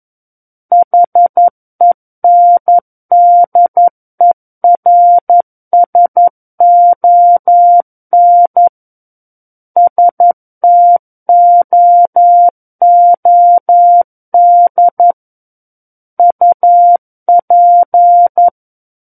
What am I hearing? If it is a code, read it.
Morse code, heard as a keyed tone, about 11 wpm, 700 Hz. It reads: HENDERSON STOOD UP
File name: Book One / 02-Chapter 2 - 11 WPM